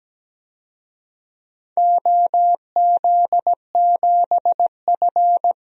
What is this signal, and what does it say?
Morse code, heard as a keyed tone, about 17 wpm, 700 Hz. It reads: OZ7F
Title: DXCC-100 Call Signs - 4 Characters - 17wpm